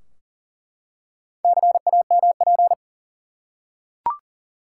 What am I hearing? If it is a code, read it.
Morse code, heard as a keyed tone, about 40 wpm, 700 Hz. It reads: CAMP
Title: Four Letter Words - 40wpm